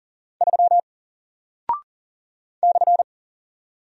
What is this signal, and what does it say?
Morse code, heard as a keyed tone, about 40 wpm, 700 Hz. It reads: 3 /
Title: Single Character (Letter:Number:Punctuation:Prosign) Alt 40wpm